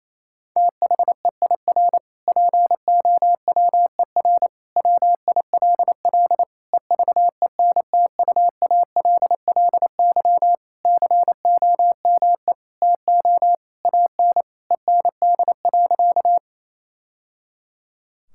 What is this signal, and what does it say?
Morse code, heard as a keyed tone, about 28 wpm, 700 Hz. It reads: THEIR POWER WILL EVENTUALLY COME TO AN END.
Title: Sentences from Top 1000 Words - Review Part 6 - 28wpm